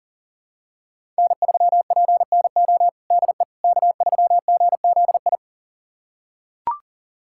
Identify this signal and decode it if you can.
Morse code, heard as a keyed tone, about 40 wpm, 700 Hz. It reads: N3PNO DE K3GZI
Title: USA Callsign DE Callsign - 5 Characters - 40wpm